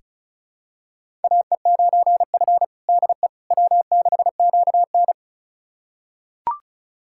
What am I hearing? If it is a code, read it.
Morse code, heard as a keyed tone, about 36 wpm, 700 Hz. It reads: AE9F DE W6QN
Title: USA Callsign DE Callsign - 4 Characters - 35wpm